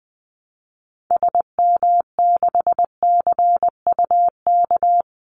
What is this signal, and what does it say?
Morse code, heard as a keyed tone, about 20 wpm, 700 Hz. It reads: SM6CUK
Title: Call Signs Repeated 2x - 20wpm